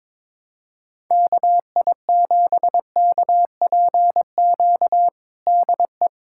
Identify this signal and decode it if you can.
Morse code, heard as a keyed tone, about 21 wpm, 700 Hz. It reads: KI7KPQ DE
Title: USA Callsign DE Callsign - 6 Characters - 22wpm